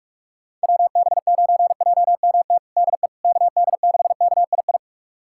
Morse code, heard as a keyed tone, about 46 wpm, 700 Hz. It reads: WB9JMT DE KD6KII
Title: USA Callsign DE Callsign - 6 Characters - 45wpm